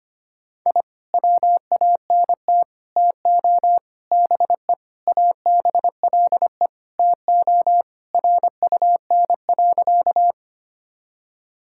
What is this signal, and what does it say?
Morse code, heard as a keyed tone, about 25 wpm, 700 Hz. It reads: I WANT TO BE ABLE TO RUN.
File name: practice / 25wpm / Sentences with 3-Word Phrases - 25wpm